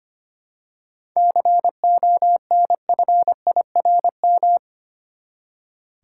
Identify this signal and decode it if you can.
Morse code, heard as a keyed tone, about 25 wpm, 700 Hz. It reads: CONFIRM
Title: Seven Letter Words - 25wpm